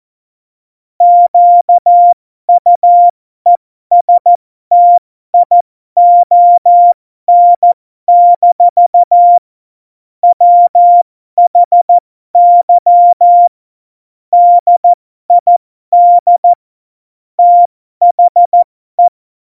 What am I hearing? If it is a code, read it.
Morse code, heard as a keyed tone, about 14 wpm, spaced as for 12 wpm, 700 Hz. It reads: QUESTION- WHY DID THE
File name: Morsekurs / Witze